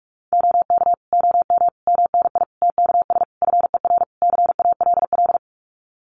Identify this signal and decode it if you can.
Morse code, heard as a keyed tone, about 44 wpm, 700 Hz. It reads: OK OM MNI TKS FER CALL
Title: Common QSO Phrases 45wpm